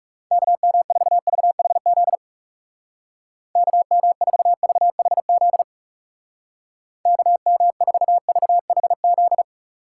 Morse code, heard as a keyed tone, about 36 wpm, 700 Hz. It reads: KM4VHZ KM4VHZ KM4VHZ
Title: Speed Racing - USA 2x3 Call Signs 30wpm